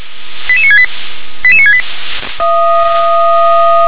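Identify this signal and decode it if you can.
Five-tone selective call (ZVEI), heard as quick runs of five tones, tones of about 70 ms. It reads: 88968 88968
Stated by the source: ZVEI five-tone call received on SDR.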